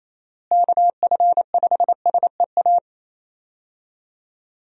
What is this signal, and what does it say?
Morse code, heard as a keyed tone, about 27 wpm, 700 Hz. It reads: KF5SEA
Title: USA Call Signs - 6 Characters - Repeated 3x - 28wpm